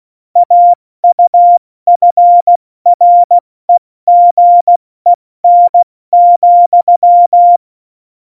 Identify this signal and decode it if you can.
Morse code, heard as a keyed tone, about 16 wpm, 700 Hz. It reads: AUFREGEN,
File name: Morsekurs / Witze / Witz 15. September 2020, 70 F80